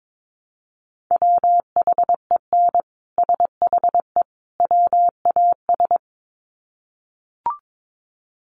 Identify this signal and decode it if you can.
Morse code, heard as a keyed tone, about 22 wpm, 700 Hz. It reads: WHEN SHE WAS
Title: Sets of 3 Words 22wpm